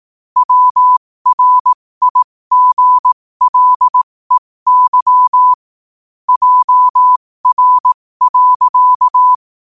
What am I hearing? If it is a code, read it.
Morse code, heard as a keyed tone, about 18 wpm, spaced as for 15 wpm, 1000 Hz. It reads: WRIGLEY JR.
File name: qotdfiles / 15wpm